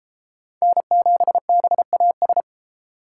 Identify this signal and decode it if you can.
Morse code, heard as a keyed tone, about 30 wpm, 700 Hz. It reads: N7BAS